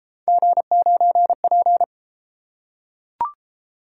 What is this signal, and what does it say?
Morse code, heard as a keyed tone, about 30 wpm, 700 Hz. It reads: G9P